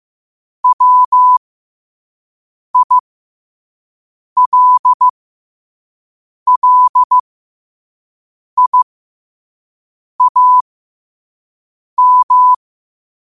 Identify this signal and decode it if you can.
Morse code, heard as a keyed tone, about 15 wpm, spaced as for 5 wpm, 1000 Hz. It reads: WILLIAM
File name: qotdfiles / 5wpm